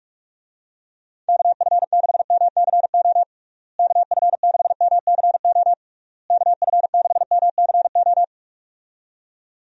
Morse code, heard as a keyed tone, about 46 wpm, 700 Hz. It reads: KF6MCO KF6MCO KF6MCO
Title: USA Call Signs - 6 Characters - Repeated 3x - 45wpm